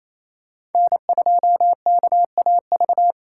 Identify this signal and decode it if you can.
Morse code, heard as a keyed tone, about 28 wpm, 700 Hz. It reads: N2KAV
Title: USA Call Signs - 5 Characters - 4x Character Spacing - 28wpm